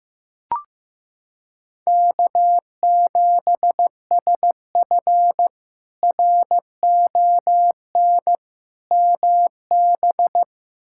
Morse code, heard as a keyed tone, about 15 wpm, 700 Hz. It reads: K7SF RON MB
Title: K1USN SST QSO Practice - 15wpm